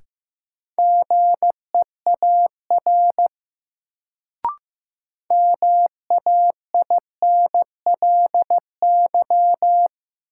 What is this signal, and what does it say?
Morse code, heard as a keyed tone, about 15 wpm, 700 Hz. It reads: GEAR MAINLY
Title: Sentences from Top 1400 Words - New Word Drill - 15wpm